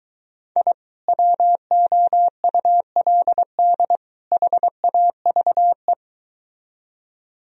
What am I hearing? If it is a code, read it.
Morse code, heard as a keyed tone, about 23 wpm, 700 Hz. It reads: I WOULD HAVE